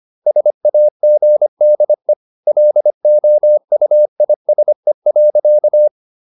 Morse code, heard as a keyed tone, about 25 wpm, 600 Hz. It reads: SAGDE LOUISE.
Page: Ferie 25 wpm